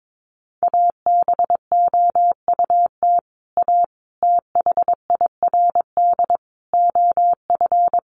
Morse code, heard as a keyed tone, about 22 wpm, 700 Hz. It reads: ABOUT A THIRD OF
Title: Sets of 5 Words Encore2 22wpm